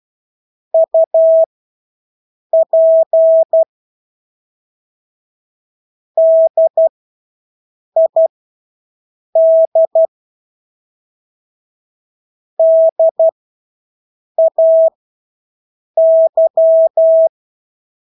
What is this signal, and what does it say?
Morse code, heard as a keyed tone, about 12 wpm, spaced as for 6 wpm, 650 Hz. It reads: UP DID DAY
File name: Engelske ord 12-6 wpm